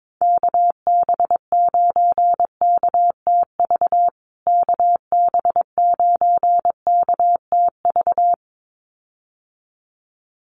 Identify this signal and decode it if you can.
Morse code, heard as a keyed tone, about 22 wpm, 700 Hz. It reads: KB9KTV KB9KTV
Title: USA Call Signs Repeated 2x 22wpm